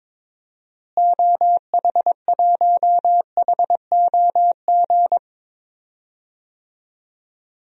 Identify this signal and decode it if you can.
Morse code, heard as a keyed tone, about 22 wpm, 700 Hz. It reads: OH1HOG
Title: DXCC-100 Call Signs - 6 Characters - Repeated 2x - 22wpm